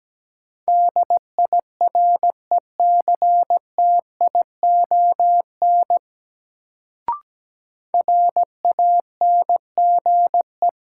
Morse code, heard as a keyed tone, about 17 wpm, 700 Hz. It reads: DIRECTION RANGE
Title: Sets of 4 Words - New Word Drill - 17wpm